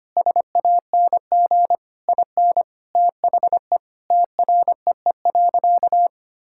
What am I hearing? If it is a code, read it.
Morse code, heard as a keyed tone, about 25 wpm, 700 Hz. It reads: SANG IN THE TREE.
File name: practice / 25wpm / Sentences with 2-Word Phrases - 25wpm